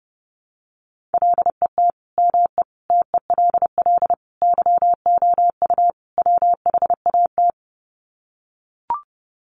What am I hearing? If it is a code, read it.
Morse code, heard as a keyed tone, about 30 wpm, 700 Hz. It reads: LET ME TELL YOU WHAT
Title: Sets of 5 Words Encore 30wpm